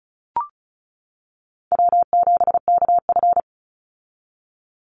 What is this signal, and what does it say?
Morse code, heard as a keyed tone, about 36 wpm, 700 Hz. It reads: W7KF
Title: Call Signs - 35wpm